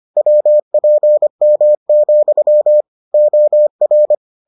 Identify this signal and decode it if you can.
Morse code, heard as a keyed tone, about 25 wpm, 600 Hz. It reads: WPM, OR